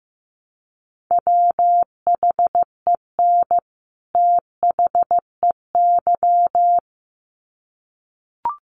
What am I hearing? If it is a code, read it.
Morse code, heard as a keyed tone, about 15 wpm, 700 Hz. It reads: WHEN THEY
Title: Sets of 2 Words 15wpm